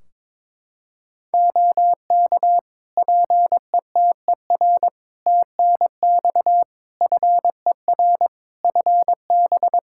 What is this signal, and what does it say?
Morse code, heard as a keyed tone, about 22 wpm, 700 Hz. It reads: OK PETER TNX FER FB
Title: Practice QSO - Exchange 4 - Ending - 22wpm